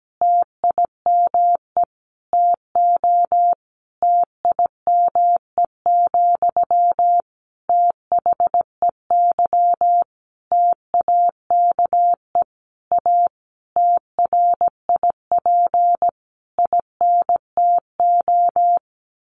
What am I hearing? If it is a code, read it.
Morse code, heard as a keyed tone, about 17 wpm, 700 Hz. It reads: TIME TO TIME, THEY TAKE A TRIP INTO
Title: Sentences from Top 1300 Words 17wpm